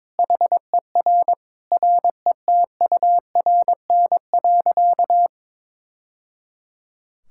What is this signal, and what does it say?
Morse code, heard as a keyed tone, about 22 wpm, 700 Hz. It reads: HER RETURN.
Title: Sentences from Top 1000 Words - Review Part 6 - 22wpm